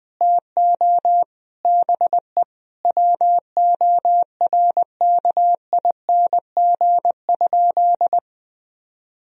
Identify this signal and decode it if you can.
Morse code, heard as a keyed tone, about 20 wpm, 700 Hz. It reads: TO BE WORKING?
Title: Sentences from Top 1000 Words - Review Part 4 - 20wpm